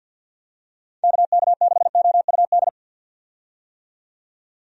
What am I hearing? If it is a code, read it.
Morse code, heard as a keyed tone, about 50 wpm, 700 Hz. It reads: KK6OUD
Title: USA Call Signs - 6 Characters - 50wpm